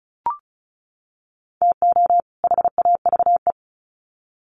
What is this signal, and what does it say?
Morse code, heard as a keyed tone, about 34 wpm, 700 Hz. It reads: TO HAVE
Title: Sets of 2 Words 35wpm